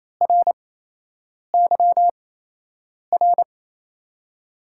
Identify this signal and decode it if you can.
Morse code, heard as a keyed tone, about 27 wpm, spaced as for 8 wpm, 700 Hz. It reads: RYR
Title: USA Call Signs - 8x Character Spacing - 28wpm